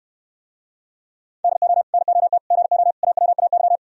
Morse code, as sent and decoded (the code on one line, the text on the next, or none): -.. --- / -. --- - / --. --- / .- .-- .- -.--
DO NOT GO AWAY